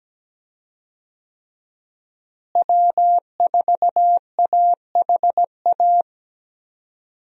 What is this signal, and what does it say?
Morse code, heard as a keyed tone, about 17 wpm, 700 Hz. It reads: W4AHA
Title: USA Call Signs - 5 Characters - Repeated 3x - 17wpm